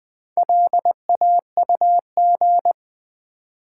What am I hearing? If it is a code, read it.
Morse code, heard as a keyed tone, about 20 wpm, 700 Hz. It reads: LAUG
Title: Build A Word - 2 Letters At A Time - 8-Letter Words - 20wpm